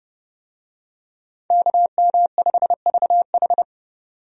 Morse code, heard as a keyed tone, about 32 wpm, 700 Hz. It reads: KM5VH
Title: USA Call Signs - 5 Characters - 30wpm